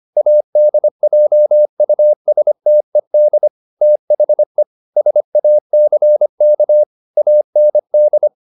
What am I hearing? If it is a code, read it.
Morse code, heard as a keyed tone, about 25 wpm, 600 Hz. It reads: ADJUSTED THE SACK AND
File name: the old man and the sea part 1 25wpm